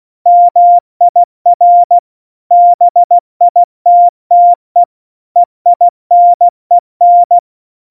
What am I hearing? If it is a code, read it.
Morse code, heard as a keyed tone, about 16 wpm, 700 Hz. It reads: MIR BITTE EINEN
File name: Morsekurs / Witze / Witz 03. September 2019, 70 F80